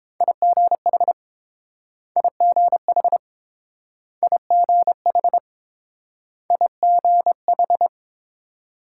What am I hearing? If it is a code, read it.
Morse code, heard as a keyed tone, about 30 wpm, 700 Hz. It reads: IGH IGH IGH IGH